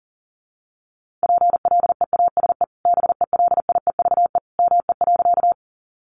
Morse code, heard as a keyed tone, about 40 wpm, 700 Hz. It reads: PLEASE BELIEVE ME.
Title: Sentences from Top 1500 Words 40wpm